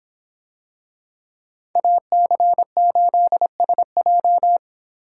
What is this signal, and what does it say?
Morse code, heard as a keyed tone, about 25 wpm, 700 Hz. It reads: AC8SJ